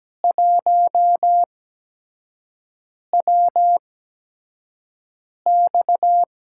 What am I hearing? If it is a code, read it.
Morse code, heard as a keyed tone, about 17 wpm, spaced as for 5 wpm, 700 Hz. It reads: 1WX
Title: USA Call Signs - 4 Characters - 8x Character Spacing - 17wpm